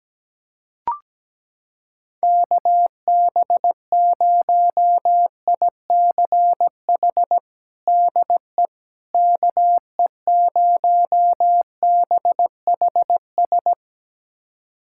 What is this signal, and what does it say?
Morse code, heard as a keyed tone, about 17 wpm, 700 Hz. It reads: KB0ICH DE KE0BHS
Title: USA Callsign DE Callsign - 6 Characters - 17wpm